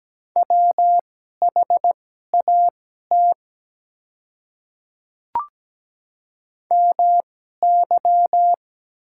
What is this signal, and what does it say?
Morse code, heard as a keyed tone, about 17 wpm, spaced as for 12 wpm, 700 Hz. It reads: WHAT MY
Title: 2x Character Spacing - Top 100 Words - 17wpm